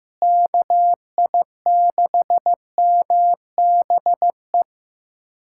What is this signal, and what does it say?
Morse code, heard as a keyed tone, about 15 wpm, 700 Hz. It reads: KI6MBE
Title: USA Callsign DE Callsign - 6 Characters - 15wpm